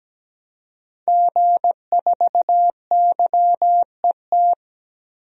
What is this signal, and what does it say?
Morse code, heard as a keyed tone, about 17 wpm, 700 Hz. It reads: G4YET
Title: DXCC-100 Call Signs - 5 Characters - 17wpm